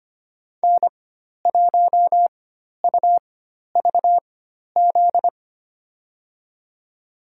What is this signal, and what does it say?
Morse code, heard as a keyed tone, about 25 wpm, spaced as for 12 wpm, 700 Hz. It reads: N1UVZ
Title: USA Call Signs - 4x Character Spacing - 25wpm